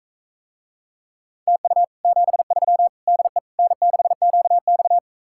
Morse code, heard as a keyed tone, about 45 wpm, 700 Hz. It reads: TU 73 DE N6QX